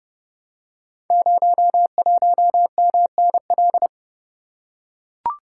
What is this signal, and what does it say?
Morse code, heard as a keyed tone, about 29 wpm, 700 Hz. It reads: 01MNL
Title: Code Groups - 5 Letter:Numbers - 30wpm